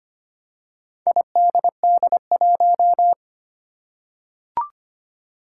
Morse code, heard as a keyed tone, about 25 wpm, 700 Hz. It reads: IDD1
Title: Four Challenging Letter-Number Combinations 25wpm